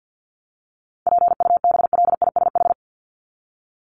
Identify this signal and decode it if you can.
Morse code, heard as a keyed tone, about 50 wpm, 700 Hz. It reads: PUBLISH
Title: ED Suffix - Top 100 Words with Base Words 50wpm